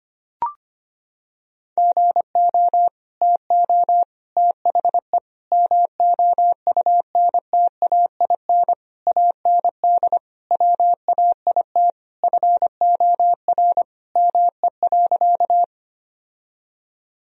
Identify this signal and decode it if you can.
Morse code, heard as a keyed tone, about 25 wpm, 700 Hz. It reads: GO TO THE MOUNTAIN AND WAIT FOR ME.
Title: Sentences from Top 1000 Words - Review Part 1 - 25wpm